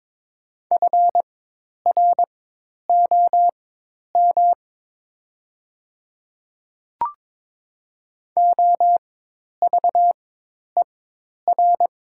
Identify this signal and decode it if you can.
Morse code, heard as a keyed tone, about 22 wpm, spaced as for 10 wpm, 700 Hz. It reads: FROM OVER
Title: 4x Character Spacing - Top 100 Words - 22wpm